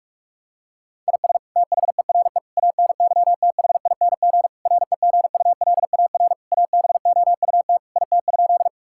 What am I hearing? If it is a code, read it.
Morse code, heard as a keyed tone, about 46 wpm, 700 Hz. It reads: IS THERE ANYTHING REGULAR ABOUT IT?
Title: Sentences from Top 1000 Words - Review Part 6 - 45wpm